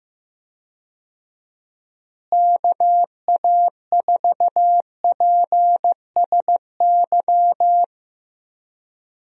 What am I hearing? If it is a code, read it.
Morse code, heard as a keyed tone, about 15 wpm, 700 Hz. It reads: KA4PSY
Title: USA Call Signs - 6 Characters - Repeated 2x - 15wpm